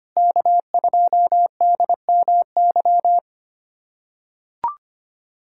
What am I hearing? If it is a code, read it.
Morse code, heard as a keyed tone, about 25 wpm, 700 Hz. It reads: K2DMY
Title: USA Callsign DE Callsign - 5 Characters - 25wpm